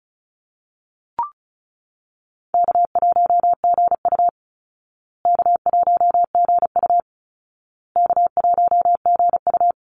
Morse code, heard as a keyed tone, about 34 wpm, 700 Hz. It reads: K1GU K1GU K1GU
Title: Call Signs Repeated 2x - 35wpm